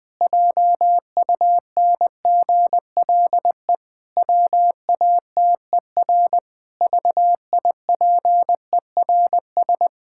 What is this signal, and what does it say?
Morse code, heard as a keyed tone, about 20 wpm, 700 Hz. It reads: JUNGLE WATER VIPERS
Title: Pangram Challenge v2 - 20wpm